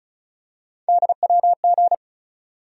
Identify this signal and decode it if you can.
Morse code, heard as a keyed tone, about 36 wpm, 700 Hz. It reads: DWG
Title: USA Call Sign Postfixes - 3 Letters 35wpm